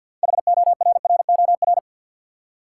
Morse code, heard as a keyed tone, about 52 wpm, 700 Hz. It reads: HORROR
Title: Six Letter Words - 50wpm